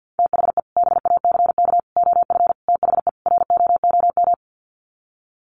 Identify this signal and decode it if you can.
Morse code, heard as a keyed tone, about 48 wpm, 700 Hz. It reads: THE BACK OF THE ROOM
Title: Sets of 5 Words Encore 50wpm